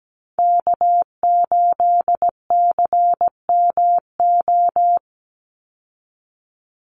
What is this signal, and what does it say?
Morse code, heard as a keyed tone, about 17 wpm, 700 Hz. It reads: K8CMO
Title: Call Signs Repeated 3x - 17wpm